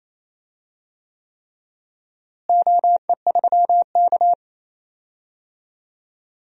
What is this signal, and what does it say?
Morse code, heard as a keyed tone, about 28 wpm, 700 Hz. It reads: OE3K
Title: DXCC-100 Call Signs - 4 Characters - Repeated 3x - 28wpm